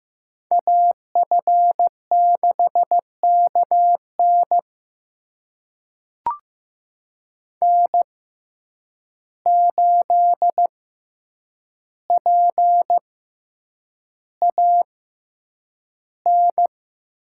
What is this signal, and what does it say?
Morse code, heard as a keyed tone, about 15 wpm, spaced as for 8 wpm, 700 Hz. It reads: AF6KN N8PAN
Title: USA Call Signs - 5 Characters - 6x Character Spacing - 15wpm